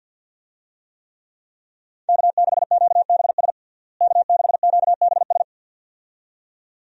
Morse code, heard as a keyed tone, about 53 wpm, 700 Hz. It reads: K6QBS K6QBS
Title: USA Call Signs - 5 Characters - Repeated 2x - 50wpm